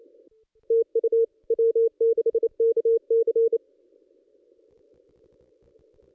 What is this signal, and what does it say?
Morse code, heard as a keyed tone, about 25 wpm, 460 Hz. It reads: TU W6KC
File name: contests / ss-cw-k5tr-op-n5rz -1800